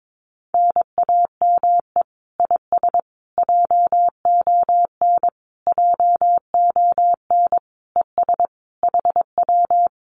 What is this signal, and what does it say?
Morse code, heard as a keyed tone, about 22 wpm, 700 Hz. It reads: NAME IS JON JON ES HW
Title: Common QSO Phrases Encore 22wpm